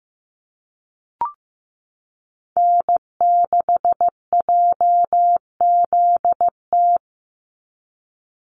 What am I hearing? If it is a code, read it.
Morse code, heard as a keyed tone, about 15 wpm, 700 Hz. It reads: N6JZT
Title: NA SOTA Activator Callsigns 15wpm